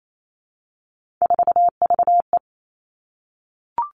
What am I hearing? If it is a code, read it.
Morse code, heard as a keyed tone, about 28 wpm, 700 Hz. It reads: 4VE
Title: Three Challenging Letter-Number Combinations 28wpm